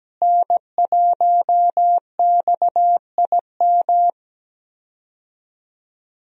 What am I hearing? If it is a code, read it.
Morse code, heard as a keyed tone, about 17 wpm, 700 Hz. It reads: N1XIM
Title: USA Call Signs - 5 Characters - Repeated 2x - 17wpm